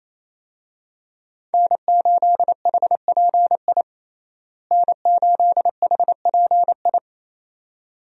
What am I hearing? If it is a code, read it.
Morse code, heard as a keyed tone, about 28 wpm, 700 Hz. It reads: N8HPI N8HPI
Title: USA Call Signs - 5 Characters - Repeated 2x - 28wpm